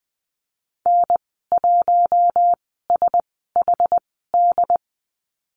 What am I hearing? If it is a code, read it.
Morse code, heard as a keyed tone, about 20 wpm, spaced as for 14 wpm, 700 Hz. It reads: N1SHD
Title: 069 Lesson 12g - Letter d - Call Signs 20wpm